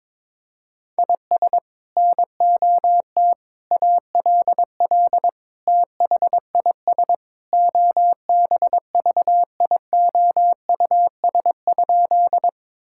Morse code, heard as a keyed tone, about 22 wpm, 700 Hz. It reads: IS NOT ALL THIS OBVIOUS?
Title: Sentences from Top 1600 Words 22wpm